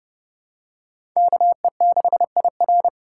Morse code, heard as a keyed tone, about 30 wpm, 700 Hz. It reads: KE6IR
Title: USA Call Signs - 5 Characters - 8x Character Spacing - 30wpm